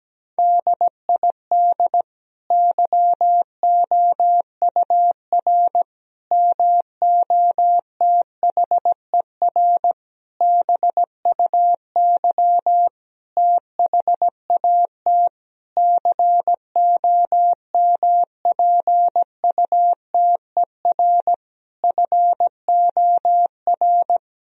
Morse code, heard as a keyed tone, about 17 wpm, 700 Hz. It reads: DID YOUR MOTHER BUY THAT COMPUTER FOR
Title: Sentences from Top 1000 Words - Review Part 3 - 17wpm